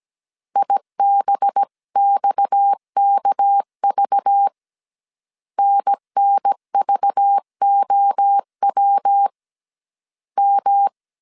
Morse code, heard as a keyed tone, about 17 wpm, 800 Hz. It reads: IBXKV NNVOW M
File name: CW / DET / 1 6 1500C S A 16wpm 2635